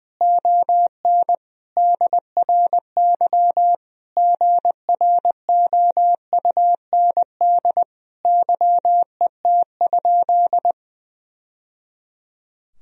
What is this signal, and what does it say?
Morse code, heard as a keyed tone, about 20 wpm, 700 Hz. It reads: ON DRY GROUND YET?
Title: Sentences from Top 1000 Words - Review Part 3 - 20wpm